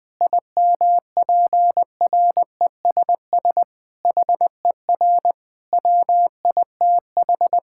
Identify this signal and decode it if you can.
Morse code, heard as a keyed tone, about 20 wpm, 700 Hz. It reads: IMPRESS HER WITH
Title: Sentences from Top 1600 Words 20wpm